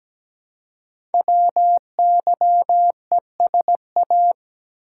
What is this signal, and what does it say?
Morse code, heard as a keyed tone, about 17 wpm, 700 Hz. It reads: WYESA
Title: Code Groups - 5 Letters - 17wpm